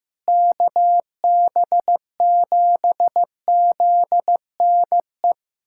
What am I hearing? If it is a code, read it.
Morse code, heard as a keyed tone, about 15 wpm, 700 Hz. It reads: KB7ZNE
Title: USA Callsign DE Callsign - 6 Characters - 15wpm